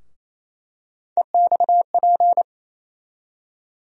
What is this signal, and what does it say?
Morse code, heard as a keyed tone, about 28 wpm, 700 Hz. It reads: EXP
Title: USA Call Sign Postfixes - 3 Letters 28wpm